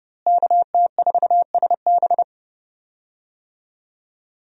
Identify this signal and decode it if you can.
Morse code, heard as a keyed tone, about 30 wpm, 700 Hz. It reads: KT4SB
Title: USA Call Signs - 5 Characters - Repeated 3x - 30wpm